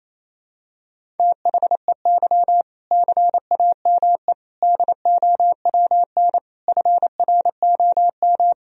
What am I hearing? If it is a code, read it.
Morse code, heard as a keyed tone, about 28 wpm, 700 Hz. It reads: THEY CAME DOWN FROM
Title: Sentences from Top 1000 Words - Review Part 1 - 28wpm